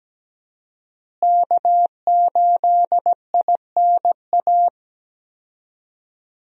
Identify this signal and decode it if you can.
Morse code, heard as a keyed tone, about 17 wpm, 700 Hz. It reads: K8INA
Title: USA Call Signs - 5 Characters - Repeated 2x - 17wpm